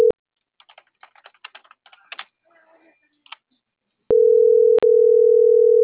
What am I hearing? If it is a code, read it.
Ringback tone (North America), 440 + 480 Hz.